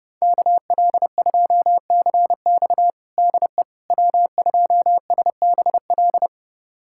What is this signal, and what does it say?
Morse code, heard as a keyed tone, about 30 wpm, 700 Hz. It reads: KL2CX DE W2SBL
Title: USA Callsign DE Callsign - 5 Characters - 30wpm